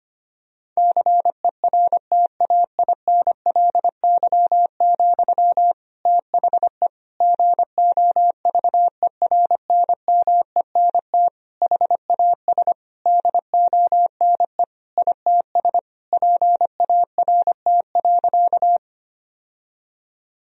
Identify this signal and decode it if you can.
Morse code, heard as a keyed tone, about 25 wpm, 700 Hz. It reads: CERTAINLY, THE GOVERNMENT HAS DONE ITS PART.
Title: Sentences from Top 1000 Words - Review Part 3 - 25wpm